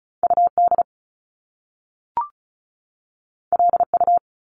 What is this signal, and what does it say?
Morse code, heard as a keyed tone, about 36 wpm, 700 Hz. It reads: UD LU
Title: Top Two Letter Combinations in Words 35wpm